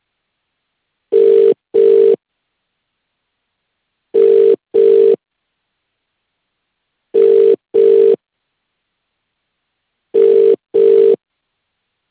Ringback tone (UK), 400 + 450 Hz.